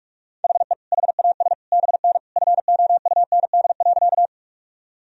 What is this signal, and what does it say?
Morse code, heard as a keyed tone, about 41 wpm, 700 Hz. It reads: HE HAS BN FOUND.
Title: Sentences from Top 300 Words and Abbreviations 45wpm